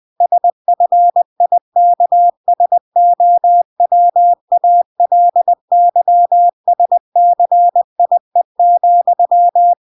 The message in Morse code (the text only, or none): SFIKSOWALYSCIE,